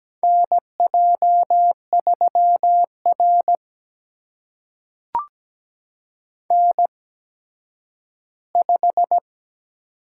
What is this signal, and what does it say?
Morse code, heard as a keyed tone, about 17 wpm, spaced as for 10 wpm, 700 Hz. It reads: NJ3R N5
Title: USA Call Signs - 4 Characters - 8x Character Spacing - 17wpm